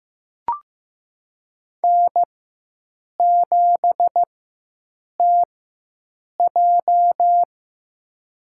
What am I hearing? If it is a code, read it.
Morse code, heard as a keyed tone, about 15 wpm, spaced as for 7 wpm, 700 Hz. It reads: N7TJ
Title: USA Call Signs - 4 Characters - 4x Character Spacing - 15wpm